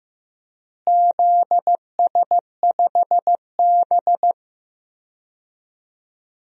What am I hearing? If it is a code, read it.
Morse code, heard as a keyed tone, about 15 wpm, 700 Hz. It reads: ZS5B
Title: DXCC-100 Call Signs - 4 Characters - Repeated 2x - 15wpm